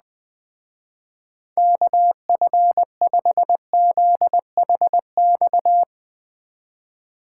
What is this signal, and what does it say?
Morse code, heard as a keyed tone, about 20 wpm, 700 Hz. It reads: KF5ZHX
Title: Speed Racing - USA 2x3 Call Signs 17wpm